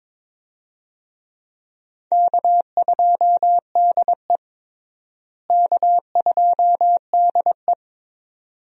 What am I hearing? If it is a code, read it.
Morse code, heard as a keyed tone, about 22 wpm, 700 Hz. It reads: K2DE K2DE
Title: USA Call Signs - 4 Characters - Repeated 2x - 22wpm